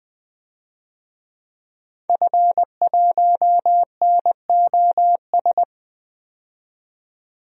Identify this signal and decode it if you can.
Morse code, heard as a keyed tone, about 20 wpm, 700 Hz. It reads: F1NOS
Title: DXCC-100 Call Signs - 5 Characters - 20wpm